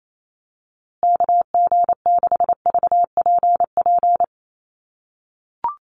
Morse code, heard as a keyed tone, about 28 wpm, 700 Hz. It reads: KG6VPP
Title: USA Call Signs Repeated 3x 28wpm